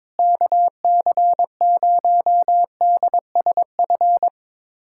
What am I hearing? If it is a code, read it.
Morse code, heard as a keyed tone, about 22 wpm, 700 Hz. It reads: KC0DSF
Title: USA Callsign DE Callsign 22wpm